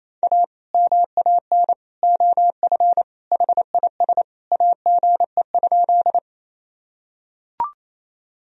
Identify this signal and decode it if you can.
Morse code, heard as a keyed tone, about 28 wpm, 700 Hz. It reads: A MAN OF HIS AGE?
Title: Sentences from Top 1000 Words - Review Part 3 - 28wpm